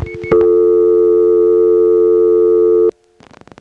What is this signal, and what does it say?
Dial tone (North America), 350 + 440 Hz.